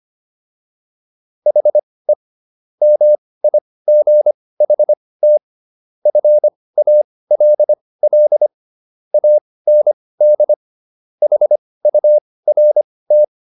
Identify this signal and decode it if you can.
Morse code, heard as a keyed tone, about 25 wpm, spaced as for 18 wpm, 600 Hz. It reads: HE MIGHT FALL AND HURT